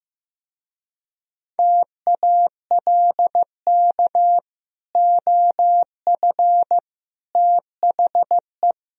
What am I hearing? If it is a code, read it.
Morse code, heard as a keyed tone, about 15 wpm, 700 Hz. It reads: TALK OF THE
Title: Speed-Racing - Sets of 5 Words 15wpm